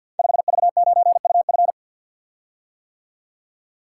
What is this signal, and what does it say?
Morse code, heard as a keyed tone, about 47 wpm, 700 Hz. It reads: 5V9UF
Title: Code Groups - 5 Letter:Numbers - 50wpm